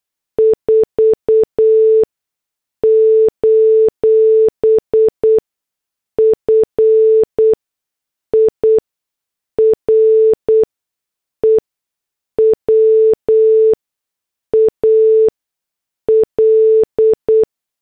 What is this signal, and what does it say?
Morse code, heard as a keyed tone, about 8 wpm, spaced as for 6 wpm, 440 Hz. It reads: 4: FIREWAL